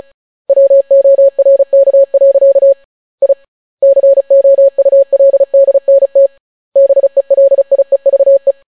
Morse code, heard as a keyed tone, about 36 wpm, 550 Hz. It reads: WORK. I COULDNT BELIEVE